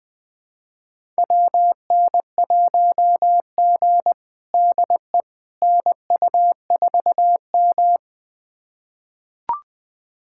Morse code, heard as a keyed tone, about 20 wpm, 700 Hz. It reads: WN1G DE NU4M
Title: USA Callsign DE Callsign - 4 Characters - 20wpm